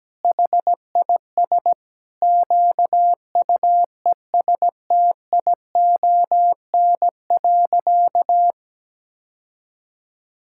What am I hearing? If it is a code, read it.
Morse code, heard as a keyed tone, about 17 wpm, 700 Hz. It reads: HIS QUESTION.
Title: Sentences from Top 1000 Words - Review Part 6 - 17wpm